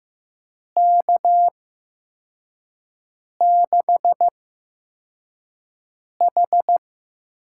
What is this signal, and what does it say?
Morse code, heard as a keyed tone, about 15 wpm, spaced as for 4 wpm, 700 Hz. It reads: K6H
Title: USA Call Signs - 6 Characters - 8x Character Spacing - 15wpm